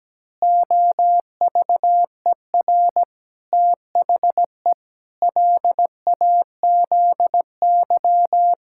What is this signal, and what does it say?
Morse code, heard as a keyed tone, about 17 wpm, 700 Hz. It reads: OVER THE LAZY
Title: Pangram Challenge v2 - 17wpm